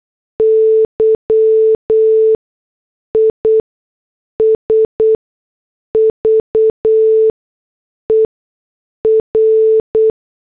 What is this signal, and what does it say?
Morse code, heard as a keyed tone, about 8 wpm, spaced as for 6 wpm, 440 Hz. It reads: Y IS VER